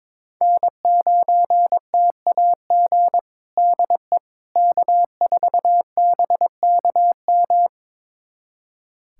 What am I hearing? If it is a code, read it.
Morse code, heard as a keyed tone, about 22 wpm, 700 Hz. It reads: N9TAG DE K4BKM
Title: USA Callsign DE Callsign - 5 Characters - 22wpm